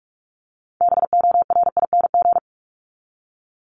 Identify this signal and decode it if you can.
Morse code, heard as a keyed tone, about 44 wpm, 700 Hz. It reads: BORING
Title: Top 1501-1750 Words 45wpm